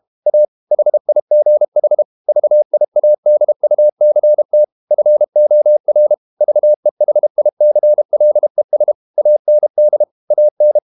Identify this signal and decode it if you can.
Morse code, heard as a keyed tone, about 32 wpm, 600 Hz. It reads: A HIGH VIADUCT FOR VEHICLES AND AN
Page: Never 31wpm | CW med Gnister